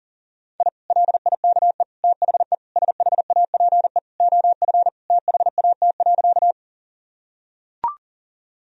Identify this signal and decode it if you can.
Morse code, heard as a keyed tone, about 40 wpm, 700 Hz. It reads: I LIKE THE SHAPE OF THAT.
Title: Sentences from Top 1000 Words - Review Part 3 - 40wpm